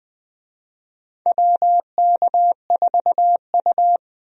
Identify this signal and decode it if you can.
Morse code, heard as a keyed tone, about 20 wpm, 700 Hz. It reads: WK4U
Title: USA Call Signs - 4 Characters - 20wpm